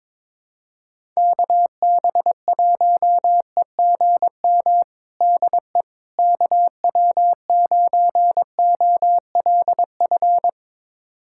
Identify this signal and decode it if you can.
Morse code, heard as a keyed tone, about 21 wpm, 700 Hz. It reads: KB1EGM DE KW9OLF
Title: USA Callsign DE Callsign 22wpm